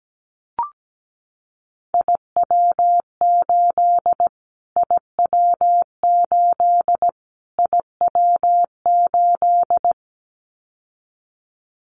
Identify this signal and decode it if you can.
Morse code, heard as a keyed tone, about 17 wpm, 700 Hz. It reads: IW8 IW8 IW8
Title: Callsign Prefixes for Top 25 DXCC - Repeated 3x - 17wpm